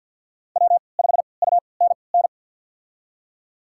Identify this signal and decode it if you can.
Morse code, heard as a keyed tone, about 53 wpm, spaced as for 29 wpm, 690 Hz. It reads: W5UNN
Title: USA Call Signs - 5 Characters - 3x Character Spacing - 50wpm